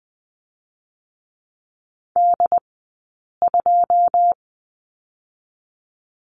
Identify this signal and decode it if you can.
Morse code, heard as a keyed tone, about 20 wpm, 700 Hz. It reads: D 2
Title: 158 Lesson 26d - Letter g - Two Characters 20wpm